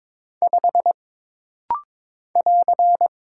Morse code, heard as a keyed tone, about 22 wpm, 700 Hz. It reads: 5 <AR>
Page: This rapid-fire timing is geared to push your ICR (Instant Character Recognition) skills.